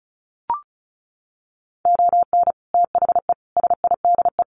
Morse code, heard as a keyed tone, about 34 wpm, 700 Hz. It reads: ON THE SIDE
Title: Sets of 5 Words 35wpm